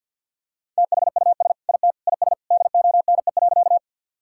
Morse code, heard as a keyed tone, about 50 wpm, 700 Hz. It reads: THUS IT IS DONE.
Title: Sentences from Top 1000 Words - Review Part 5 - 50wpm